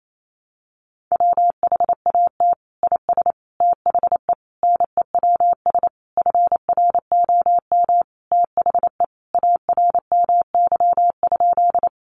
Morse code, heard as a keyed tone, about 28 wpm, 700 Hz. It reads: WHAT IS THE NEWS FROM THE ARMY?
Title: Sentences from Top 800 Words 28wpm